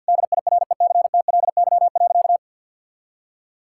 Morse code, heard as a keyed tone, about 50 wpm, 700 Hz. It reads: DIRECTLY.
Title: Sentences from Top 1000 Words - Review Part 5 - 50wpm